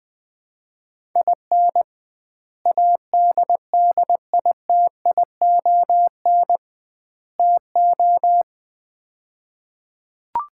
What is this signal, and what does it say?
Morse code, heard as a keyed tone, about 21 wpm, 700 Hz. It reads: IN ADDITION TO
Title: 2x Word-Spacing - Sets of 3 Words - 20wpm